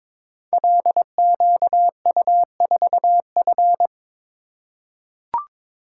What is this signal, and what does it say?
Morse code, heard as a keyed tone, about 22 wpm, 700 Hz. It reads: LQU4F
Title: Code Groups - 5 Letter:Numbers - 22wpm